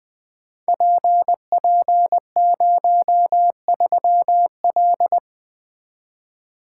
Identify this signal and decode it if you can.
Morse code, heard as a keyed tone, about 20 wpm, 700 Hz. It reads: PP03L
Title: Code Groups - 5 Letter:Numbers - 20wpm